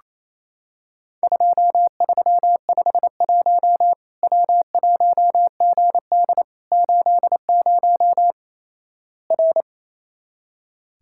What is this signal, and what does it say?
Morse code, heard as a keyed tone, about 27 wpm, 700 Hz. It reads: 2351 W1GD 80 R
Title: Worked All Europe DX Contest - QTC Exchange - 28wpm